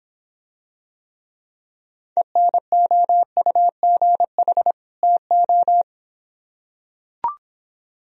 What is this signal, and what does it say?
Morse code, heard as a keyed tone, about 26 wpm, 700 Hz. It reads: ENOUGH TO